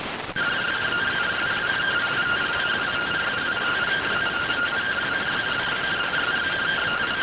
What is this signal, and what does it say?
Two-tone teleprinter signal, about 170 Hz shift.